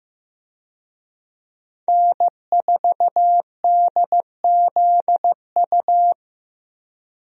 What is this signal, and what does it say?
Morse code, heard as a keyed tone, about 15 wpm, 700 Hz. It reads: N4DZU
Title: USA Call Signs - 5 Characters - Repeated 2x - 15wpm